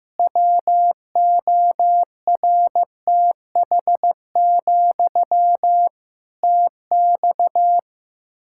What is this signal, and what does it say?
Morse code, heard as a keyed tone, about 15 wpm, 700 Hz. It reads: WORTH, TX
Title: Practice QSO - Exchange 1 - Beginning - 15wpm